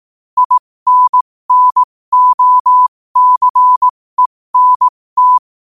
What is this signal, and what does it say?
Morse code, heard as a keyed tone, about 18 wpm, spaced as for 15 wpm, 1000 Hz. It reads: INNOCENT
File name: qotdfiles / 15wpm